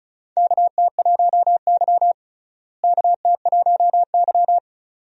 Morse code, heard as a keyed tone, about 36 wpm, 700 Hz. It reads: KT1Y KT1Y
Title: USA Call Signs - 4 Characters - Repeated 2x - 35wpm